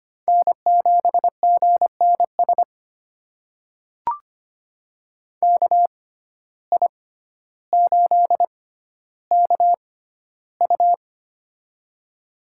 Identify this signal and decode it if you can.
Morse code, heard as a keyed tone, about 25 wpm, spaced as for 13 wpm, 700 Hz. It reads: N7GNS KI8KU
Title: USA Call Signs - 5 Characters - 6x Character Spacing - 25wpm